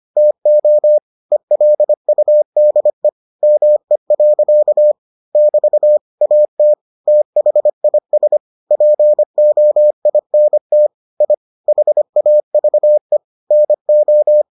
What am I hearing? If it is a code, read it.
Morse code, heard as a keyed tone, about 25 wpm, 600 Hz. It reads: TO ELUDE ME. <BT> AT THIS POINT I HAVE NO